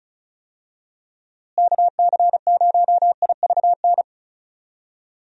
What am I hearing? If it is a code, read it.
Morse code, heard as a keyed tone, about 34 wpm, 700 Hz. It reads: KC0IVN
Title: USA Call Signs - 6 Characters - Repeated 2x - 35wpm